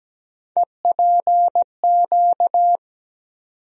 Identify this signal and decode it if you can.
Morse code, heard as a keyed tone, about 17 wpm, 700 Hz. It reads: EPQ
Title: USA Call Sign Postfixes - 3 Letters 17wpm